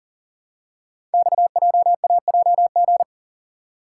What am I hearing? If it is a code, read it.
Morse code, heard as a keyed tone, about 38 wpm, 700 Hz. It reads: XJAJG
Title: Code Groups - 5 Letters - 40wpm